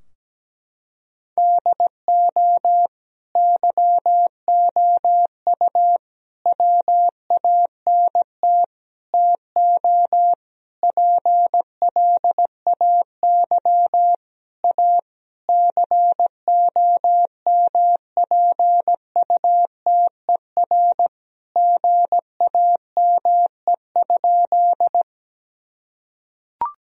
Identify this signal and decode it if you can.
Morse code, heard as a keyed tone, about 17 wpm, 700 Hz. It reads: DO YOU WANT TO PLAY A COMPUTER GAME?
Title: Sentences from Top 1000 Words - Review Part 3 - 17wpm